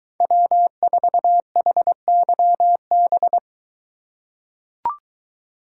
Morse code, heard as a keyed tone, about 23 wpm, 700 Hz. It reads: W4HYB